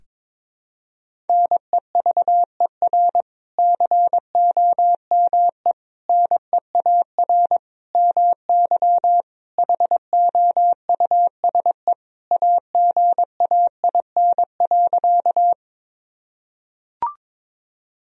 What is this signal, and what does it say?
Morse code, heard as a keyed tone, about 22 wpm, 700 Hz. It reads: NEVER COME NEAR MY HOUSE AGAIN.
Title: Sentences from Top 1000 Words - Review Part 1 - 22wpm